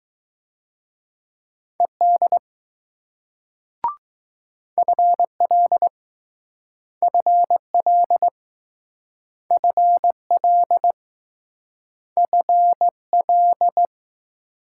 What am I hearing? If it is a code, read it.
Morse code, heard as a keyed tone, about 20 wpm, 700 Hz. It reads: ED FL FL FL FL